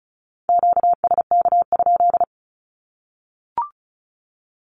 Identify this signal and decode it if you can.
Morse code, heard as a keyed tone, about 34 wpm, 700 Hz. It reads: QSK?
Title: Q-Codes Repeated 3x 35wpm